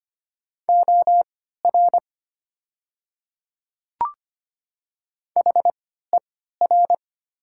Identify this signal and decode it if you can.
Morse code, heard as a keyed tone, about 25 wpm, spaced as for 14 wpm, 700 Hz. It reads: OR HER
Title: 3x Character Spacing - Top 100 Words - 25wpm